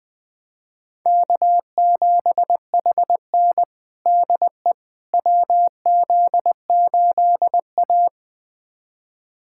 Morse code, heard as a keyed tone, about 20 wpm, 700 Hz. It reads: K7HN DE WZ8A
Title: USA Callsign DE Callsign - 4 Characters - 20wpm